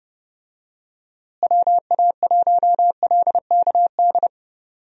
Morse code, heard as a keyed tone, about 30 wpm, 700 Hz. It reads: WA1LKD
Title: USA Call Signs - 6 Characters - Repeated 3x - 30wpm